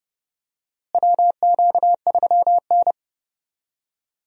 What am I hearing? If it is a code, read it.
Morse code, heard as a keyed tone, about 30 wpm, 700 Hz. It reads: WQ3N
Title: USA Call Signs - 4 Characters - Repeated 3x - 30wpm